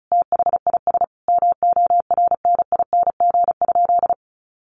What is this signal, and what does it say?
Morse code, heard as a keyed tone, about 34 wpm, 700 Hz. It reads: THIS MORNING?
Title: Sentences from Top 1200 Words 35wpm